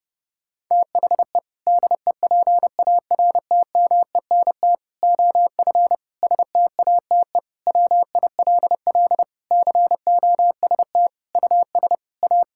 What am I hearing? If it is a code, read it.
Morse code, heard as a keyed tone, about 30 wpm, 700 Hz. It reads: THE DEPARTMENT OF STATE WILL COST US A
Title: Sentences from Top 1000 Words - Review Part 5 - 30wpm